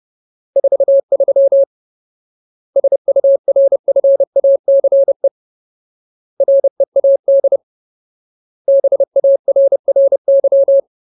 Morse code, heard as a keyed tone, about 30 wpm, 550 Hz. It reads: 43 SURFACE READ BARRY
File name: Ultimate Word List / 30WPM Ultimate